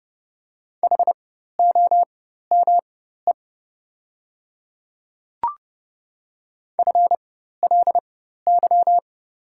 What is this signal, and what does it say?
Morse code, heard as a keyed tone, about 30 wpm, spaced as for 14 wpm, 690 Hz. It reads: HOME FLY
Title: 4x Character Spacing - Top 500 Words - 30wpm